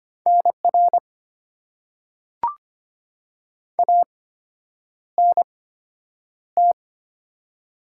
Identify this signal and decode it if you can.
Morse code, heard as a keyed tone, about 25 wpm, spaced as for 9 wpm, 690 Hz. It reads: NR ANT
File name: Top 90 QSO Elements - 8x Character Spacing - 25wpm